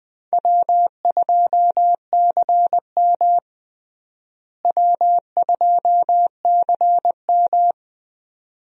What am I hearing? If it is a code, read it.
Morse code, heard as a keyed tone, about 20 wpm, 700 Hz. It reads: W2CM W2CM
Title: USA Call Signs - 4 Characters - Repeated 2x - 20wpm